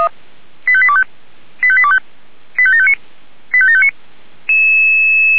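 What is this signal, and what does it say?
Five-tone selective call (ZVEI), heard as quick runs of five tones, tones of about 70 ms. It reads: 85825 85825 85759 85759
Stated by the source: ZVEI five-tone selective call received via SDR.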